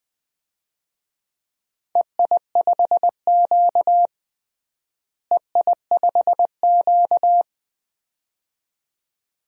Morse code, heard as a keyed tone, about 20 wpm, 700 Hz. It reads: EI5Q EI5Q
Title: DXCC-100 Call Signs - 4 Characters - Repeated 2x - 20wpm